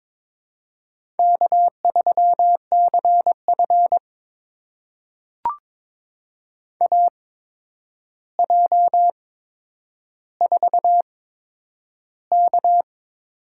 Morse code, heard as a keyed tone, about 22 wpm, spaced as for 9 wpm, 700 Hz. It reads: K3CF AJ4K
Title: USA Call Signs - 4 Characters - 8x Character Spacing - 22wpm